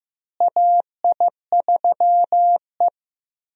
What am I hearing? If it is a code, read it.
Morse code, heard as a keyed tone, about 16 wpm, 700 Hz. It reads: AI3E
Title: USA Call Signs - 4 Characters - Repeated 3x - 15wpm